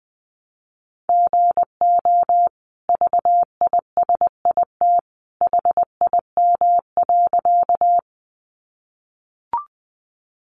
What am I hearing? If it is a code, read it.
Morse code, heard as a keyed tone, about 20 wpm, 700 Hz. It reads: GO VISIT HIM.
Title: Sentences from Top 900 Words 20wpm